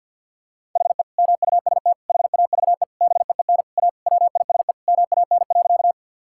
Morse code, heard as a keyed tone, about 50 wpm, 700 Hz. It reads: HE MUST HAVE BEEN A WISE MAN.
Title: Sentences from Top 1600 Words 50wpm